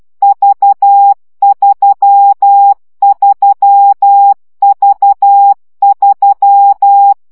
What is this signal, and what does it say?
Morse code, heard as a keyed tone, about 12 wpm, 800 Hz. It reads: V33V3
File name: CW gyakorlas v 3 10 baud